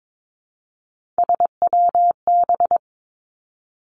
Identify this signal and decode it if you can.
Morse code, heard as a keyed tone, about 22 wpm, 700 Hz. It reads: SWB
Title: Three Challenging Letter-Number Combinations 22wpm